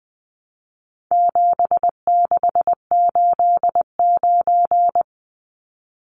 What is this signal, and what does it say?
Morse code, heard as a keyed tone, about 20 wpm, 700 Hz. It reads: 7689
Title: Numbers - 4-Digit 20wpm